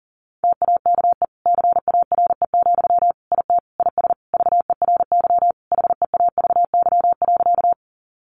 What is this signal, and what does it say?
Morse code, heard as a keyed tone, about 37 wpm, 700 Hz. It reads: TAKE CARE, IT IS VERY HEAVY.
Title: Sentences from Top 1100 Words 40wpm